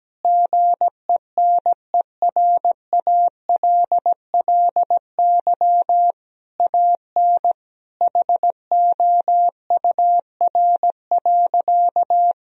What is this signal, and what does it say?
Morse code, heard as a keyed tone, about 17 wpm, 700 Hz. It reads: GENERALLY AN HOUR.
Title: Sentences from Top 1000 Words - Review Part 4 - 17wpm